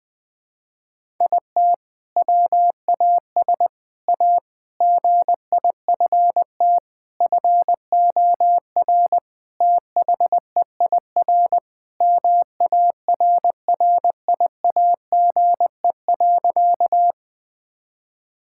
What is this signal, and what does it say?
Morse code, heard as a keyed tone, about 20 wpm, 700 Hz. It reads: IT WAS A GIFT FOR THEIR MARRIAGE.
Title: Sentences from Top 1000 Words - Review Part 6 - 20wpm